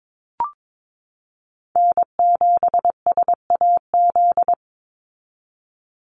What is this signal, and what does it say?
Morse code, heard as a keyed tone, about 22 wpm, 700 Hz. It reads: N7SAZ
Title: USA Call Signs 22wpm